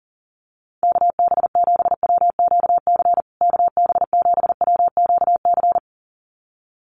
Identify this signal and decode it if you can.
Morse code, heard as a keyed tone, about 40 wpm, 700 Hz. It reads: KB7WQC KB7WQC
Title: USA Call Signs Repeated 2x 40wpm